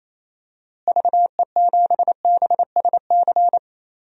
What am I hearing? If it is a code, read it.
Morse code, heard as a keyed tone, about 27 wpm, 700 Hz. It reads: VE7BSC
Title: DXCC-100 Call Signs - 6 Characters - Repeated 2x - 28wpm